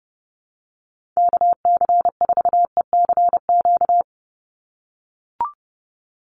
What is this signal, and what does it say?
Morse code, heard as a keyed tone, about 30 wpm, 700 Hz. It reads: KC4ECQ
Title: USA Call Signs 30wpm